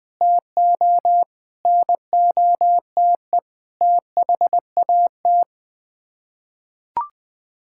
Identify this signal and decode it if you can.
Morse code, heard as a keyed tone, about 20 wpm, 700 Hz. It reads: TO NOTE THAT
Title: Sets of 6 Words - 20wpm